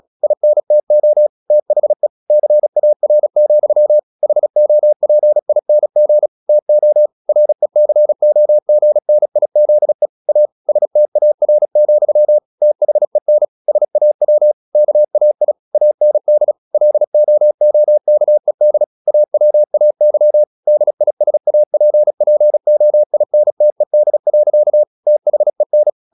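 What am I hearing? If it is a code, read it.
Morse code, heard as a keyed tone, about 36 wpm, 600 Hz. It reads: INTO THE CAR, HOPING TO RECOGNIZE A STAR, THEN SAW KAI AND LOOKED AWAY DISAPPOINTED. THEN
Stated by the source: Morse practice at 36 wpm, text of Ken Follett's novel Never, from CW med Gnister.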